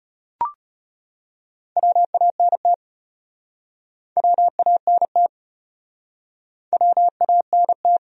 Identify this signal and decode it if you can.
Morse code, heard as a keyed tone, about 34 wpm, 700 Hz. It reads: WANT WANT WANT